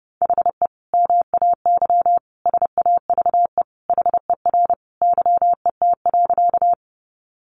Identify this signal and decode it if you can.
Morse code, heard as a keyed tone, about 30 wpm, 700 Hz. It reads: HE MAY SAVE HER YET.
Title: Sentences from Top 1100 Words 30wpm